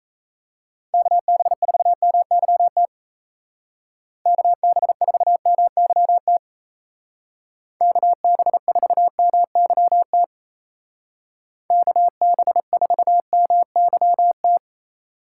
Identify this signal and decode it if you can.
Morse code, heard as a keyed tone, about 34 wpm, 700 Hz. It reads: KB4MYT KB4MYT KB4MYT KB4MYT
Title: Speed Racing - USA 2x3 Call Signs 28wpm